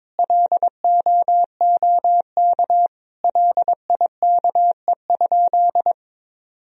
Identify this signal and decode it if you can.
Morse code, heard as a keyed tone, about 22 wpm, 700 Hz. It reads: LOOK LIKE?
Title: Sentences from Top 100 Words and Abbreviations 22wpm